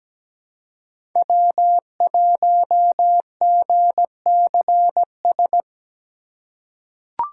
Morse code, heard as a keyed tone, about 17 wpm, 700 Hz. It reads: W1GCS
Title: K1USN SST QSO Practice - 17wpm